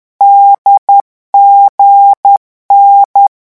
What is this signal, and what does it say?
Morse code, heard as a keyed tone, about 10 wpm, 800 Hz. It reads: DGN
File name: morse mp3 / lecon4-vitesse 10